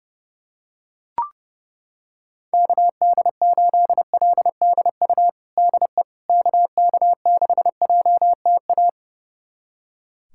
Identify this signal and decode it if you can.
Morse code, heard as a keyed tone, about 32 wpm, 700 Hz. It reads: KD8LDU DE KK6JTA
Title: USA Callsign DE Callsign - 6 Characters - 30wpm